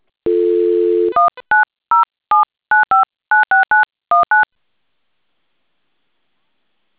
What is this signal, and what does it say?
DTMF, heard as telephone touch tones, tones of about 150 ms. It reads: 19079596919